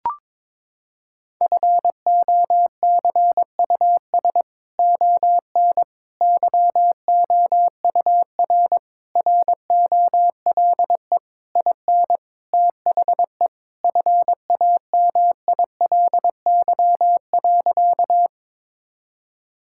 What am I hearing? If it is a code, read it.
Morse code, heard as a keyed tone, about 22 wpm, 700 Hz. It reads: FOCUS ON YOUR ROLE IN THE FAMILY.
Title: Sentences from Top 1000 Words - Review Part 6 - 22wpm